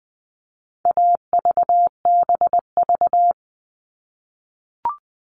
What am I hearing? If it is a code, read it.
Morse code, heard as a keyed tone, about 20 wpm, 700 Hz. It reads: AVBV
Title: All Cut Numbers - 4-Digit 20wpm